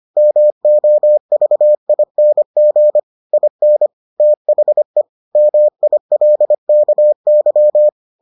Morse code, heard as a keyed tone, about 25 wpm, 600 Hz. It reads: MOVING IN THE MILKY